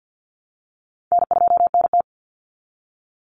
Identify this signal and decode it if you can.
Morse code, heard as a keyed tone, about 48 wpm, 700 Hz. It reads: N2NT
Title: Easy Call Signs Repeated 2x - 50wpm